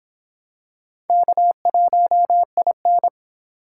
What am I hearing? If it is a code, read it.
Morse code, heard as a keyed tone, about 26 wpm, 700 Hz. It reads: K1IN